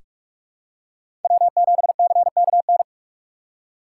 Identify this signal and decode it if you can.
Morse code, heard as a keyed tone, about 45 wpm, 700 Hz. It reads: W7CKN
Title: USA Call Signs - 5 Characters - 2x Character Spacing - 45wpm